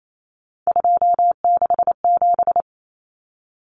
Morse code, heard as a keyed tone, about 28 wpm, 700 Hz. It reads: 267
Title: Common Cut Numbers - 3-Digit 28wpm